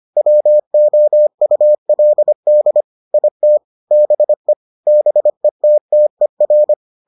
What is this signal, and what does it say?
Morse code, heard as a keyed tone, about 25 wpm, 600 Hz. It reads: WOULD IT BE BETTER